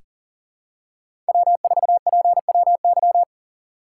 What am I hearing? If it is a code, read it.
Morse code, heard as a keyed tone, about 40 wpm, 700 Hz. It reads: W4PWY
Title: USA Call Signs - 5 Characters - 4x Character Spacing - 40wpm